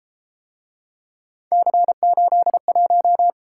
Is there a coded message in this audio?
Yes: Morse code from 1.5 to 3.3 s, heard as a keyed tone, about 32 wpm, 700 Hz. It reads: C81